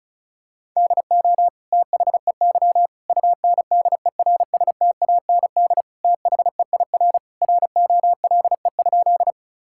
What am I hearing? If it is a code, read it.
Morse code, heard as a keyed tone, about 34 wpm, 700 Hz. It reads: DO THEY UNDERSTAND THEIR ROLE?
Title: Sentences from Top 1000 Words - Review Part 6 - 35wpm